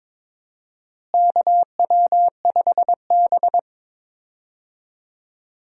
Morse code, heard as a keyed tone, about 22 wpm, 700 Hz. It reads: KW5B
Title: USA Call Signs - 4 Characters - Repeated 3x - 22wpm